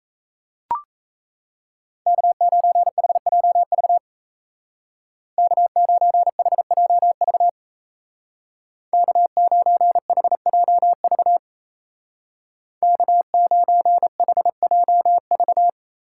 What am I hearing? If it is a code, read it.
Morse code, heard as a keyed tone, about 37 wpm, 700 Hz. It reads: K9HJV K9HJV K9HJV K9HJV